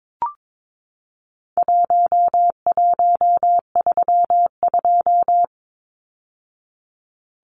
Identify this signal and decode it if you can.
Morse code, heard as a keyed tone, about 22 wpm, 700 Hz. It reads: 1132
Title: Numbers - Contest Style - 4-Digit 22wpm